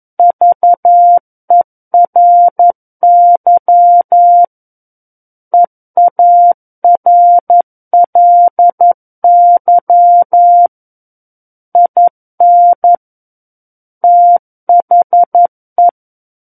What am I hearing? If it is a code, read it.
Morse code, heard as a keyed tone, about 11 wpm, 700 Hz. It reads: VERY EARLY IN THE
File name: 11 WPM